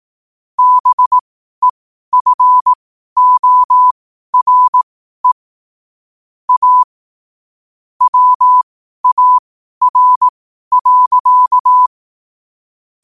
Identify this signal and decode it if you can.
Morse code, heard as a keyed tone, about 18 wpm, spaced as for 12 wpm, 1000 Hz. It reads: BEFORE A WAR.
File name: qotdfiles / 12wpm